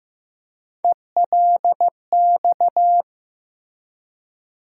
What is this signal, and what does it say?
Morse code, heard as a keyed tone, about 15 wpm, 700 Hz. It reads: ELX
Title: USA Call Sign Postfixes - 3 Letters 15wpm